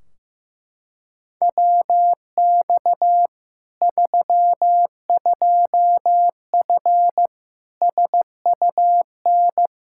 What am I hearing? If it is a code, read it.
Morse code, heard as a keyed tone, about 15 wpm, 700 Hz. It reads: WX 32F SUN
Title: Practice QSO - Exchange 2 - Info - 15wpm